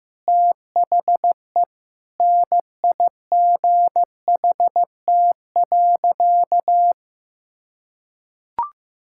Morse code, heard as a keyed tone, about 15 wpm, 700 Hz. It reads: THE NIGHT.
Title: Sentences from Top 1600 Words 15wpm